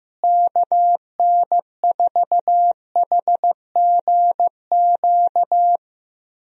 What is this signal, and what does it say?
Morse code, heard as a keyed tone, about 15 wpm, 700 Hz. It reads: KN4HGQ
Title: USA Call Signs - 6 Characters - Repeated 3x - 15wpm